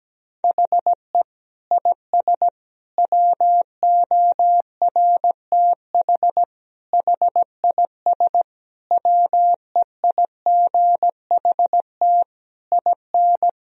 Morse code, heard as a keyed tone, about 17 wpm, 700 Hz. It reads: HE IS WORTH HIS WEIGHT IN
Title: Sentences from Top 1000 Words - Review Part 3 - 17wpm